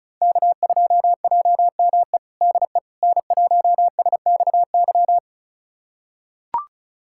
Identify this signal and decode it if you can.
Morse code, heard as a keyed tone, about 36 wpm, 700 Hz. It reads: K2JME DE N1SXY
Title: USA Callsign DE Callsign - 5 Characters - 35wpm